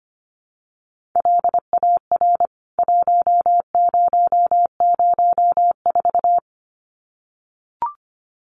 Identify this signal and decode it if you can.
Morse code, heard as a keyed tone, about 25 wpm, 700 Hz. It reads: LAR 1004
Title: CWT Exchange 25wpm